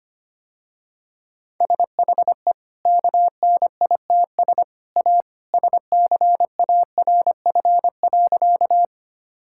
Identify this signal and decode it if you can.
Morse code, heard as a keyed tone, about 25 wpm, 700 Hz. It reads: SHE KNITS A SCARF.
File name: Easy Sentences with Hints - 25wpm